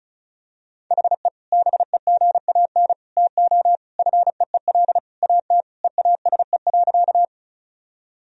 Morse code, heard as a keyed tone, about 34 wpm, 700 Hz. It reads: HE BEGAN TO FEEL AT EASE.
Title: Sentences from Top 1000 Words - Review Part 1 - 35wpm